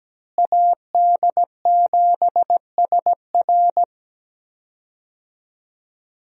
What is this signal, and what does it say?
Morse code, heard as a keyed tone, about 17 wpm, 700 Hz. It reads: AD7SR
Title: USA Call Signs - 5 Characters - Repeated 3x - 17wpm